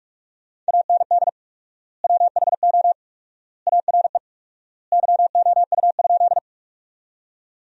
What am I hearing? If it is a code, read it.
Morse code, heard as a keyed tone, about 45 wpm, 700 Hz. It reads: AND WHO ARE YOU?
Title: 4x Word-Spacing - Sentences from Top 100 Words - 45wpm